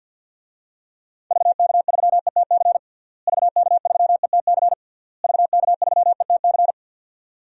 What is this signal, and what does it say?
Morse code, heard as a keyed tone, about 51 wpm, 700 Hz. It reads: VK3ETC VK3ETC VK3ETC
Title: DXCC-100 Call Signs - 6 Characters - Repeated 3x - 50wpm